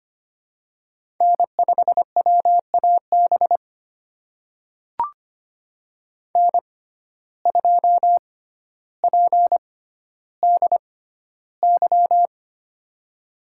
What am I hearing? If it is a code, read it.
Morse code, heard as a keyed tone, about 25 wpm, spaced as for 13 wpm, 700 Hz. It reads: N5WAB N2PDY
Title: USA Call Signs - 5 Characters - 6x Character Spacing - 25wpm